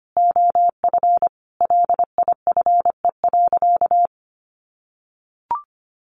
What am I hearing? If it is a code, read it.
Morse code, heard as a keyed tone, about 25 wpm, 700 Hz. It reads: OF LIFE.
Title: Sentences from Top 1500 Words 25wpm